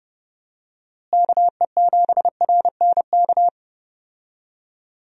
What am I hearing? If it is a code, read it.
Morse code, heard as a keyed tone, about 30 wpm, 700 Hz. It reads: KE7RNK
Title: USA Call Signs - 6 Characters - Repeated 2x - 30wpm